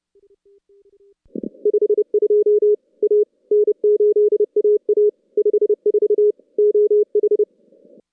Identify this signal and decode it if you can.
Morse code, heard as a keyed tone, about 30 wpm, 420 Hz. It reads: 52 A N8AA 54 OH
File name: contests / CW-SS-k5tr- -2200